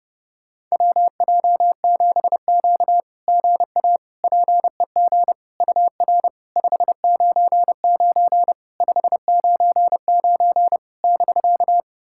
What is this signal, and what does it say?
Morse code, heard as a keyed tone, about 30 wpm, 690 Hz. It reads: WJ7Q GA PEG UR 599 599 <BK>
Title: SOTA QSO Practice 30wpm